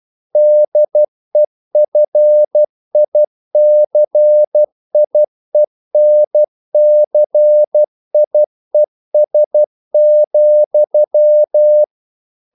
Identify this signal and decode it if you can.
Morse code, heard as a keyed tone, about 12 wpm, 600 Hz. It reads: DEFICIENCIES,
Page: English quotations in CW (telegraphy - morse)